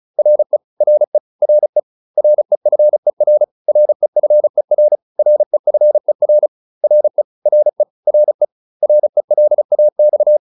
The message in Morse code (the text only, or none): RE RE RE REFER REFER REFER RE RE RE RELAX